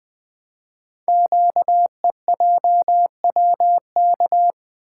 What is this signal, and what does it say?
Morse code, heard as a keyed tone, about 20 wpm, 700 Hz. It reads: QEJWK
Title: Code Groups - 5 Letters - 20wpm